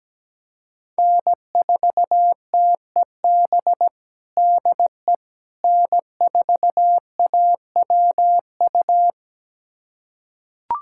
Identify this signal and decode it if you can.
Morse code, heard as a keyed tone, about 17 wpm, 700 Hz. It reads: N4TEB DE N4AWU
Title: USA Callsign DE Callsign - 5 Characters - 17wpm